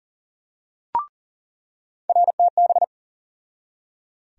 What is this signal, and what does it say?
Morse code, heard as a keyed tone, about 40 wpm, 700 Hz. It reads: RTB
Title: USA Call Sign Postfixes - 3 Letters 40wpm